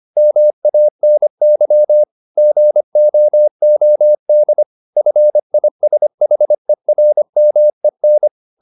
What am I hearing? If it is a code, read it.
Morse code, heard as a keyed tone, about 25 wpm, 600 Hz. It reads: MANY GOOD FISHERMEN
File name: the old man and the sea part 1 25wpm